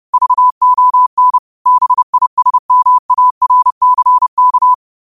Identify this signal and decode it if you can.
Morse code, heard as a keyed tone, about 30 wpm, 1000 Hz. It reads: VON BISMARCK
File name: qotdfiles / 30wpm